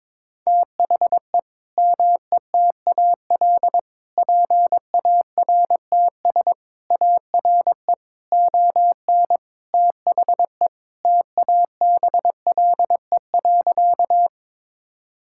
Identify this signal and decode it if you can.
Morse code, heard as a keyed tone, about 22 wpm, 700 Hz. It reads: THE METAL PARTS ARE ON THE TABLE.
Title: Sentences from Top 1000 Words - Review Part 6 - 22wpm